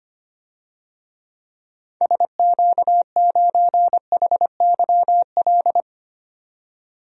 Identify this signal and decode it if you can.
Morse code, heard as a keyed tone, about 25 wpm, 700 Hz. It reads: SQ9HYL
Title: DXCC-100 Call Signs - 6 Characters - 25wpm